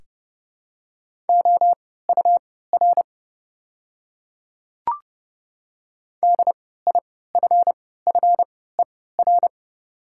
Morse code, heard as a keyed tone, about 30 wpm, spaced as for 17 wpm, 690 Hz. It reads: OUR DIFFER
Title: 3x Character Spacing - Top 200 Words - 30wpm